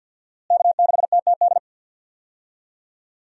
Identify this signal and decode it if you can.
Morse code, heard as a keyed tone, about 53 wpm, 700 Hz. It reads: K6TTD
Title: USA Call Signs - 5 Characters - Repeated 2x - 50wpm